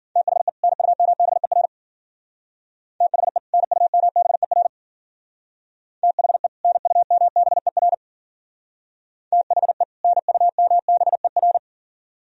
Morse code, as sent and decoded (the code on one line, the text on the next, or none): - .... / -. ..- -- -... .-. / - .... / -. ..- -- -... .-. / - .... / -. ..- -- -... .-. / - .... / -. ..- -- -... .-.
THE NUMBER THE NUMBER THE NUMBER THE NUMBER